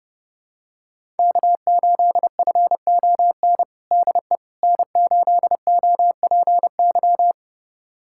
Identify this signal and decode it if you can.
Morse code, heard as a keyed tone, about 30 wpm, 700 Hz. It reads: K8FON DE N8OPY
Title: USA Callsign DE Callsign - 5 Characters - 30wpm